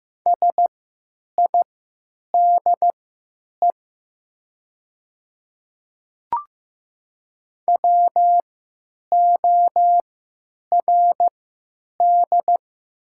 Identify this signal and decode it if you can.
Morse code, heard as a keyed tone, about 15 wpm, spaced as for 8 wpm, 700 Hz. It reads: SIDE WORD
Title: 3x Character Spacing - Top 300 Words - 15wpm